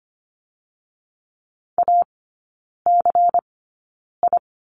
Morse code, heard as a keyed tone, about 25 wpm, spaced as for 9 wpm, 700 Hz. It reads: ACI
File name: Lesson 17e - Letter u - Three Characters 25wpm